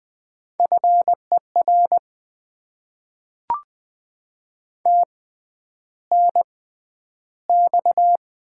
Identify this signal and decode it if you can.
Morse code, heard as a keyed tone, about 20 wpm, spaced as for 10 wpm, 700 Hz. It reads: FER TNX
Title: Top 20 QSO Elements - 6x Character Spacing - 20wpm